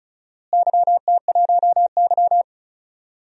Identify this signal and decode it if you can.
Morse code, heard as a keyed tone, about 34 wpm, 700 Hz. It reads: YT1Y
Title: DXCC-100 Call Signs - 4 Characters - Repeated 2x - 35wpm